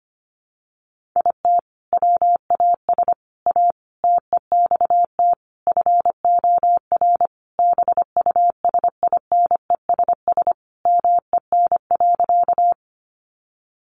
Morse code, heard as a keyed tone, about 25 wpm, 700 Hz. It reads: IT WAS A TEXT FOR BUSINESS MEN.
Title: Sentences from Top 1200 Words 25wpm